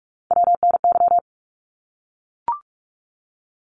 Morse code, heard as a keyed tone, about 44 wpm, 700 Hz. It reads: WNY
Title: ARRL Sections 45wpm